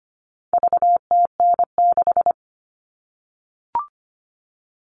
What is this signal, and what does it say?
Morse code, heard as a keyed tone, about 25 wpm, 700 Hz. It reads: VTN6
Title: All Cut Numbers - 4-Digit 25wpm